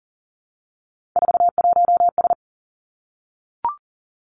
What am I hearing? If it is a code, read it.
Morse code, heard as a keyed tone, about 42 wpm, 690 Hz. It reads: 41S
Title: Three Challenging Letter-Number Combinations 40wpm